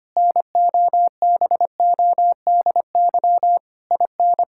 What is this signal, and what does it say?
Morse code, heard as a keyed tone, about 25 wpm, 700 Hz. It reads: NOBODY IN
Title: Sentences from Top 1600 Words 25wpm